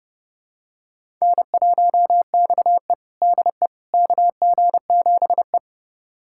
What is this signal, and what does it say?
Morse code, heard as a keyed tone, about 31 wpm, 700 Hz. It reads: N1XE DE KG7E
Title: USA Callsign DE Callsign - 4 Characters - 30wpm